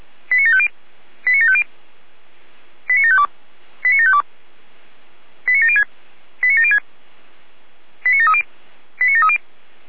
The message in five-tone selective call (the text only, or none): 89750 89750 89752 89752 89706 89706 89730 89730